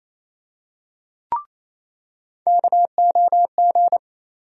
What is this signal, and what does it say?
Morse code, heard as a keyed tone, about 28 wpm, 700 Hz. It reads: KOG
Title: USA Call Sign Postfixes - 3 Letters 28wpm